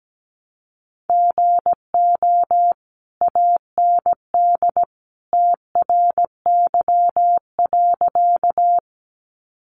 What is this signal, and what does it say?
Morse code, heard as a keyed tone, about 17 wpm, 700 Hz. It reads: GO AND TRY.
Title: Sentences from Top 200 Words 17wpm